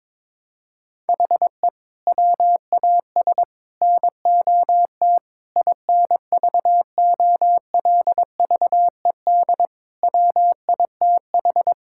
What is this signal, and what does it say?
Morse code, heard as a keyed tone, about 21 wpm, 700 Hz. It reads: HE WAS NOT INVOLVED WITH
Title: Sentences from Top 1000 Words - Review Part 5 - 22wpm